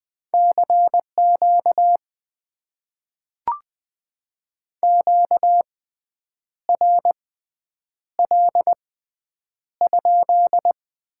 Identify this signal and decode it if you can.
Morse code, heard as a keyed tone, about 20 wpm, spaced as for 8 wpm, 700 Hz. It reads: CQ QRL?
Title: Top 90 QSO Elements - 6x Character Spacing - 20wpm